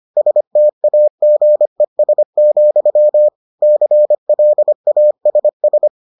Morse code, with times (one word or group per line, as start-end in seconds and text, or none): 0.2-3.3 STAGES,
3.6-5.9 CLASS